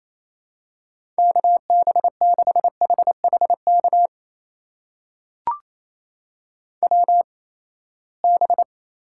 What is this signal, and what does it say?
Morse code, heard as a keyed tone, about 28 wpm, spaced as for 19 wpm, 700 Hz. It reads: KB6HHK WB
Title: USA Call Signs - 6 Characters - 8x Character Spacing - 28wpm